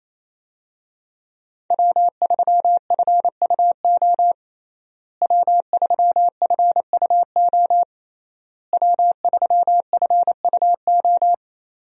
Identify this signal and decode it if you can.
Morse code, heard as a keyed tone, about 28 wpm, 700 Hz. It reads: W3FUO W3FUO W3FUO
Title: USA Call Signs - 5 Characters - Repeated 3x - 28wpm